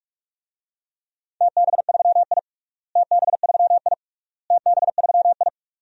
Morse code, heard as a keyed tone, about 41 wpm, 700 Hz. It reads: TB3I TB3I TB3I
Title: DXCC-100 Call Signs - 4 Characters - Repeated 3x - 45wpm